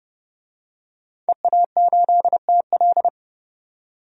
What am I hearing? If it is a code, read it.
Morse code, heard as a keyed tone, about 30 wpm, 700 Hz. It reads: EA8TL
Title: DXCC-100 Call Signs - 5 Characters - 30wpm